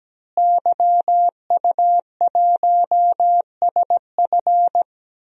Morse code, heard as a keyed tone, about 17 wpm, 700 Hz. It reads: YU1SF
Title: DXCC-100 Call Signs - 5 Characters - Repeated 2x - 17wpm